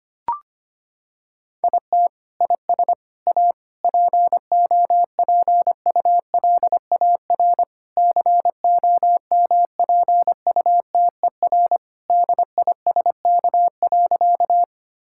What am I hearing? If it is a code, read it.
Morse code, heard as a keyed tone, about 25 wpm, 700 Hz. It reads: IT IS A POPULAR COMPUTER DISK.
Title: Sentences from Top 1000 Words - Review Part 6 - 25wpm